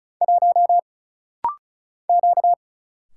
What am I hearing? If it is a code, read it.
Morse code, heard as a keyed tone, about 36 wpm, 700 Hz. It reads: 1 Q
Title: Single Letter-Number - Rapid-Fire 35wpm